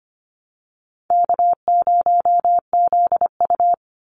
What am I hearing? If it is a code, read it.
Morse code, heard as a keyed tone, about 25 wpm, 700 Hz. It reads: K0ZU
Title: Easy Call Signs Repeated 3x - 25wpm